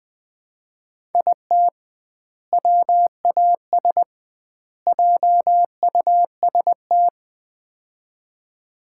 Morse code, heard as a keyed tone, about 20 wpm, 700 Hz. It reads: IT WAS JUST
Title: 2x Word-Spacing - Sets of 3 Words - 20wpm